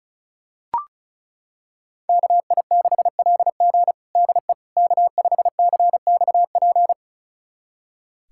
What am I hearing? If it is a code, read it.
Morse code, heard as a keyed tone, about 36 wpm, 700 Hz. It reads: KI6LG DE K5CXP
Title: USA Callsign DE Callsign - 5 Characters - 35wpm